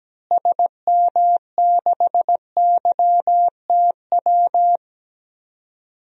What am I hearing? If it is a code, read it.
Morse code, heard as a keyed tone, about 17 wpm, 700 Hz. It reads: SM6YTW
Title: DXCC-100 Call Signs - 6 Characters - Repeated 2x - 17wpm